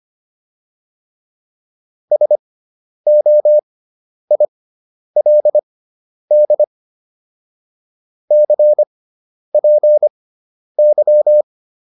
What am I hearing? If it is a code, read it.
Morse code, heard as a keyed tone, about 25 wpm, spaced as for 10 wpm, 600 Hz. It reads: SOILD CPY
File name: QSO306 10